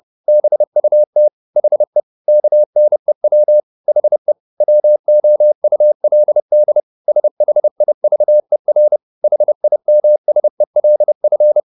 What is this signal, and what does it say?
Morse code, heard as a keyed tone, about 30 wpm, 600 Hz. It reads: BUT HE KNEW HE WOULD SHIVER HIMSELF
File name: the old man and the sea / the old man and the sea 1 30wpm